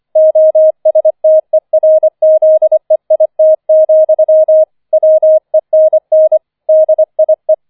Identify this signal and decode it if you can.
Morse code, heard as a keyed tone, about 25 wpm, 630 Hz. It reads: OSTERZEIT, WENN DIE